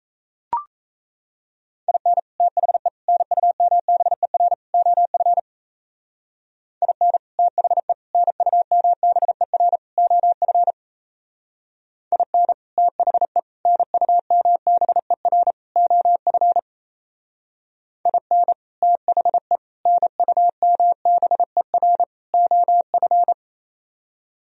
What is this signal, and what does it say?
Morse code, heard as a keyed tone, about 34 wpm, 700 Hz. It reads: IN THE NUMBER OF IN THE NUMBER OF IN THE NUMBER OF IN THE NUMBER OF
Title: Speed-Racing - Sets of 4 Words 28wpm